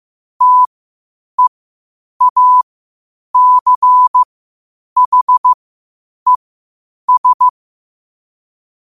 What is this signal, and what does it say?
Morse code, heard as a keyed tone, about 15 wpm, spaced as for 8 wpm, 1000 Hz. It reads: TEACHES
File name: qotdfiles / 8wpm